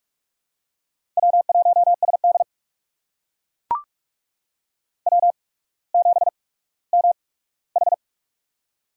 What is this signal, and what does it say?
Morse code, heard as a keyed tone, about 44 wpm, spaced as for 19 wpm, 700 Hz. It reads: W1SD W7MH
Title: USA Call Signs - 4 Characters - 8x Character Spacing - 45wpm